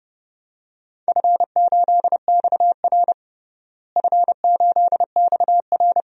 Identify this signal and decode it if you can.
Morse code, heard as a keyed tone, about 30 wpm, 700 Hz. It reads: F8XR F8XR
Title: DXCC-100 Call Signs - 4 Characters - Repeated 2x - 30wpm